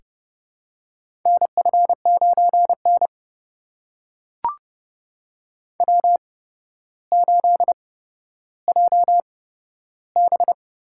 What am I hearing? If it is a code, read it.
Morse code, heard as a keyed tone, about 30 wpm, spaced as for 13 wpm, 700 Hz. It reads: NF9N W8JB
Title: USA Call Signs - 4 Characters - 8x Character Spacing - 30wpm